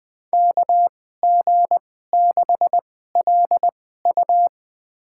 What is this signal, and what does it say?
Morse code, heard as a keyed tone, about 20 wpm, spaced as for 14 wpm, 700 Hz. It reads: KG6LU
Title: USA Call Signs - 5 Characters - 2x Character Spacing - 20wpm